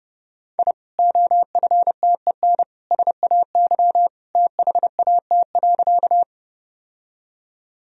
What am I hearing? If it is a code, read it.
Morse code, heard as a keyed tone, about 32 wpm, 700 Hz. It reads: I OFTEN SAY THAT.
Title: Sentences from Top 1000 Words - Review Part 1 - 30wpm